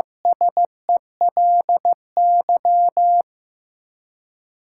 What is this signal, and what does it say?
Morse code, heard as a keyed tone, about 15 wpm, 700 Hz. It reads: SELY
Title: Seven Letter Words - 15wpm